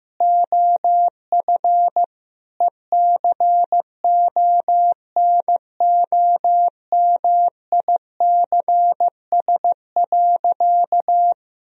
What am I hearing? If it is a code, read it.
Morse code, heard as a keyed tone, about 15 wpm, 700 Hz. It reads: OF ECONOMICS.
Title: Sentences from Top 1000 Words - Review Part 3 - 15wpm